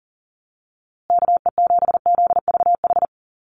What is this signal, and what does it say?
Morse code, heard as a keyed tone, about 40 wpm, 700 Hz. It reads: KE7ZVH
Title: USA Call Signs Repeated 3x 40wpm